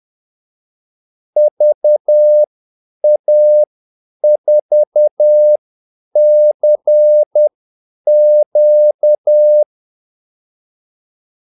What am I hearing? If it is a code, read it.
Morse code, heard as a keyed tone, about 10 wpm, spaced as for 8 wpm, 600 Hz. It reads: VA4CQ
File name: Callsigns-10-8wpm